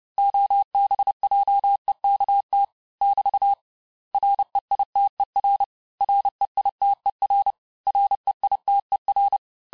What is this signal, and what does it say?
Morse code, heard as a keyed tone, about 30 wpm, 800 Hz. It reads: OBJEKT <BT> REITER REITER REITER
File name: cours-du-mercredi / audios / cours-854-mots-vitesse-28-mpm